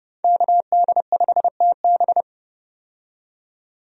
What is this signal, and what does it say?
Morse code, heard as a keyed tone, about 29 wpm, 700 Hz. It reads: KD5TB
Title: USA Call Signs - 5 Characters - 30wpm